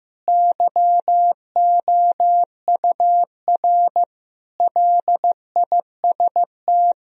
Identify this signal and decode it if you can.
Morse code, heard as a keyed tone, about 15 wpm, 700 Hz. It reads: YOUR LIST
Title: Sentences from Top 1000 Words - Review Part 2 - 15wpm